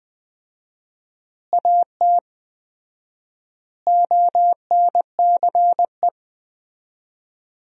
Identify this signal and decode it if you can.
Morse code, heard as a keyed tone, about 20 wpm, 700 Hz. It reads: AT ONCE
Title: 4x Word-Spacing - Sentences from Top 300 Words - 20wpm